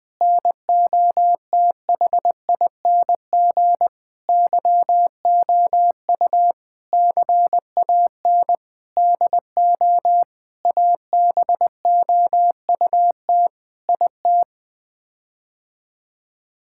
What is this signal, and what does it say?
Morse code, heard as a keyed tone, about 20 wpm, 700 Hz. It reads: NOTHING YOU CAN DO ABOUT IT
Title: Sets of 8 Words - 20wpm